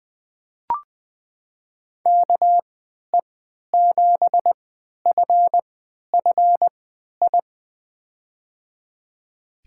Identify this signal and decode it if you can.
Morse code, heard as a keyed tone, about 20 wpm, spaced as for 11 wpm, 700 Hz. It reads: KE7FFI
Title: USA Call Signs - 3x Character Spacing - 20wpm